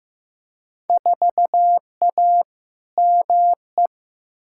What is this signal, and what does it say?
Morse code, heard as a keyed tone, about 15 wpm, 700 Hz. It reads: 4A ME
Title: Field Day Exchanges - 15wpm